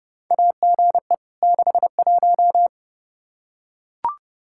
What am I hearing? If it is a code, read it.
Morse code, heard as a keyed tone, about 32 wpm, 700 Hz. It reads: AGE 61
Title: Practice QSO - Exchange 3 - More Info - 30wpm